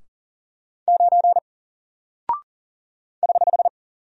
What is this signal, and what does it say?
Morse code, heard as a keyed tone, about 38 wpm, 700 Hz. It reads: 9 <HH>
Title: Single Character (Letter:Number:Punctuation:Prosign) 40wpm